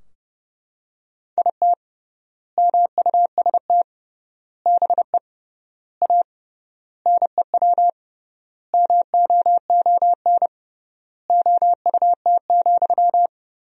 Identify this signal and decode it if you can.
Morse code, heard as a keyed tone, about 32 wpm, 700 Hz. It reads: IT MUST BE A NEW MOON OUT,
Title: 3x Word-Spacing - Sentences from Top 500 Words - 30wpm